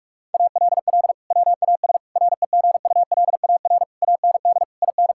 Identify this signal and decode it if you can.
Morse code, heard as a keyed tone, about 46 wpm, 700 Hz. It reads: ALL WAS REGULAR AND IN
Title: Sentences from Top 1000 Words - Review Part 6 - 45wpm